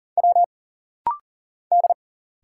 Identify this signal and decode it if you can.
Morse code, heard as a keyed tone, about 40 wpm, 700 Hz. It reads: W D
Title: Single Letters - Rapid-Fire 40wpm